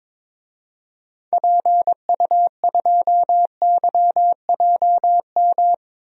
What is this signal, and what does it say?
Morse code, heard as a keyed tone, about 22 wpm, 700 Hz. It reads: PU2YJM
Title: DXCC-100 Call Signs - 6 Characters - 22wpm